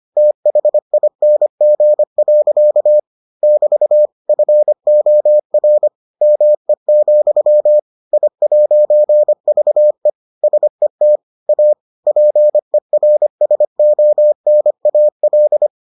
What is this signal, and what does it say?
Morse code, heard as a keyed tone, about 25 wpm, 600 Hz. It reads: THING. <BT> FOR ME, I'VE SET A PERSONAL